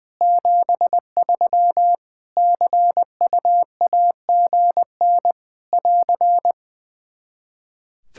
Morse code, heard as a keyed tone, about 20 wpm, 700 Hz. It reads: 73 CUAGN <AR>
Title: Practice QSO - Exchange 4 - Ending - 20wpm